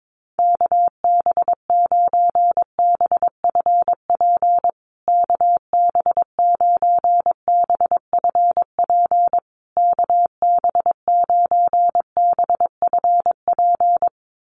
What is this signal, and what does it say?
Morse code, heard as a keyed tone, about 22 wpm, 700 Hz. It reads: KB9BFP KB9BFP KB9BFP
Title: USA Call Signs Repeated 3x 22wpm